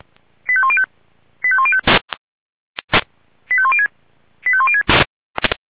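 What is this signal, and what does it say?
Five-tone selective call (ZVEI), heard as quick runs of five tones, tones of about 70 ms. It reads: 85105 85105 85106 85106
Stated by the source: ZVEI five-tone selective call received by SDR.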